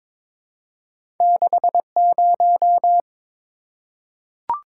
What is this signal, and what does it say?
Morse code, heard as a keyed tone, about 22 wpm, 700 Hz. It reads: 60
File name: practice / 22wpm / Two Letter-Number Combinations - 22wpm